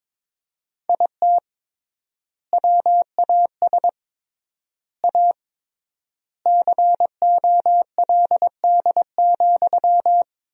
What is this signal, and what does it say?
Morse code, heard as a keyed tone, about 22 wpm, 700 Hz. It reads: IT WAS A COLD,
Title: 3x Word-Spacing - Sentences from Top 500 Words - 22wpm